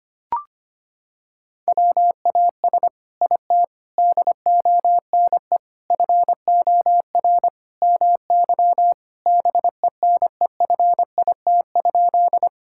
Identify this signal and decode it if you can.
Morse code, heard as a keyed tone, about 25 wpm, 700 Hz. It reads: WAS IT DONE FOR MY BENEFIT?
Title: Sentences from Top 1000 Words - Review Part 6 - 25wpm